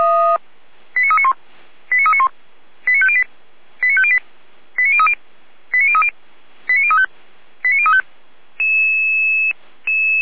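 Five-tone selective call (ZVEI), heard as quick runs of five tones, tones of about 70 ms. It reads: 89391 89391 89508 89508 89030 89030 89035 89035
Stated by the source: Five-tone ZVEI selective call received on SDR.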